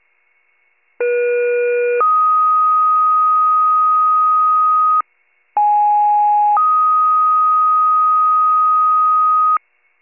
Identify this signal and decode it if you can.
Two-tone page, heard as a one-second tone, then a three-second tone, about 480 then 1250 Hz, about 830 then 1250 Hz. It is heard from 1.0 to 9.6 s.
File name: pager / 94 RESCUE / wcalert